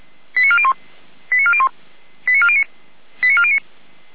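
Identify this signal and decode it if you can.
Five-tone selective call (ZVEI), heard as quick runs of five tones, tones of about 70 ms. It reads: 89401 89401 89409 89409
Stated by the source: ZVEI five-tone call received on an SDR.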